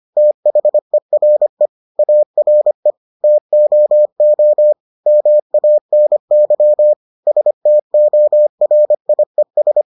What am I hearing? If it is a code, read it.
Morse code, heard as a keyed tone, about 25 wpm, 600 Hz. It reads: THERE ARE TOO MANY STORIES